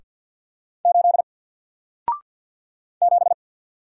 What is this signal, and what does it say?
Morse code, heard as a keyed tone, about 48 wpm, 700 Hz. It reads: 8 7
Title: Single Character (Letter:Number:Punctuation:Prosign) Alt 50wpm